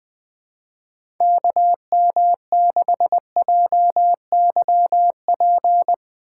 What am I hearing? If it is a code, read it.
Morse code, heard as a keyed tone, about 20 wpm, 700 Hz. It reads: KM6JYP
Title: USA Call Signs - 6 Characters - 4x Character Spacing - 20wpm